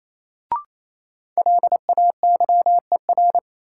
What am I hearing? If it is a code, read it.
Morse code, heard as a keyed tone, about 28 wpm, 700 Hz. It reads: LAYER
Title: Sentences from Top 1600 Words - New Word Drill Alt - 28wpm